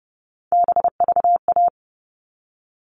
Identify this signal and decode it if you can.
Morse code, heard as a keyed tone, about 30 wpm, 700 Hz. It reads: BVA
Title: All Cut Numbers - 3-Digit 30wpm